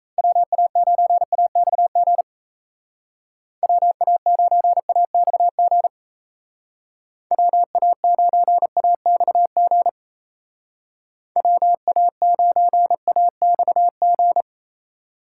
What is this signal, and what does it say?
Morse code, heard as a keyed tone, about 36 wpm, 700 Hz. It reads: WA9AXG WA9AXG WA9AXG WA9AXG